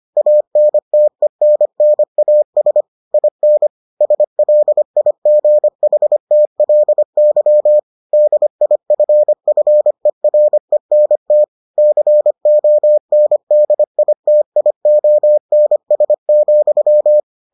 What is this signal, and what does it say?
Morse code, heard as a keyed tone, about 25 wpm, 600 Hz. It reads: ANTENNAS IN SLIGHTLY DIFFERENT CONDITIONS,